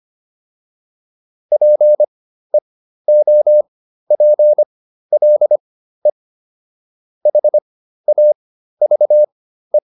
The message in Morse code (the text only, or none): PEOPLE HAVE